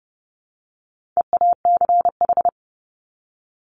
Morse code, heard as a keyed tone, about 30 wpm, 700 Hz. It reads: EACH
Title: Top 200 Words 30wpm